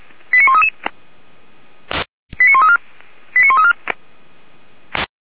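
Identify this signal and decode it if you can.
Five-tone selective call (ZVEI), heard as quick runs of five tones, tones of about 80 ms. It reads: 89133 89134 89134
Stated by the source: ZVEI five-tone selective call received on an SDR by feuerwehr-orlamuende.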